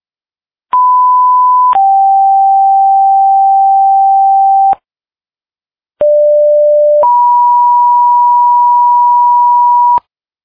Two-tone page, heard as a one-second tone, then a three-second tone, about 1000 then 780 Hz, about 590 then 970 Hz. It is heard from 0.7 to 10.0 s.